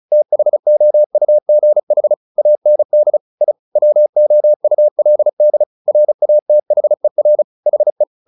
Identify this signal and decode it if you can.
Morse code, heard as a keyed tone, about 34 wpm, 600 Hz. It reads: THOUGH AND I WOULD RATHER HE
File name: the old man and the sea 2 35wpm